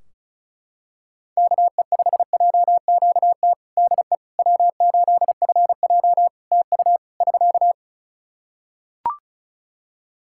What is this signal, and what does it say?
Morse code, heard as a keyed tone, about 36 wpm, 700 Hz. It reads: KE5JQT DE W8FJ TU <SK>
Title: Practice QSO - Exchange 4 - Ending - 35wpm